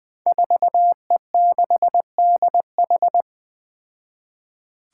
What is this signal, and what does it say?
Morse code, heard as a keyed tone, about 20 wpm, 700 Hz. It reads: 4E6DH
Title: Five Challenging Letter-Number Combinations 20wpm